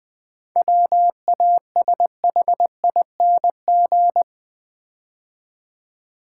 Morse code, heard as a keyed tone, about 20 wpm, 690 Hz. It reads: WASHING
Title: 7 Letter Words - First 3 Letters Given - 20wpm